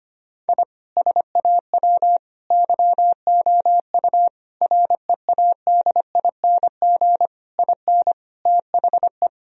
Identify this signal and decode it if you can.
Morse code, heard as a keyed tone, about 25 wpm, 700 Hz. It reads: I SAW YOU READING IN THE
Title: Sentences from Top 1000 Words - Review Part 4 - 25wpm